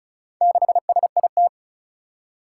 Morse code, heard as a keyed tone, about 36 wpm, 700 Hz. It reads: 6SIT
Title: Four Challenging Letter-Number Combinations 35wpm